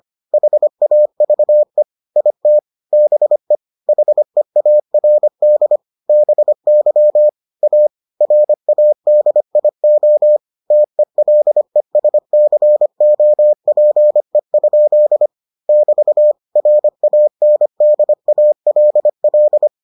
Morse code, heard as a keyed tone, about 25 wpm, 600 Hz. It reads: HAVE IT BE HEARD BY A RADIO TELESCOPE? <BT> RANDALL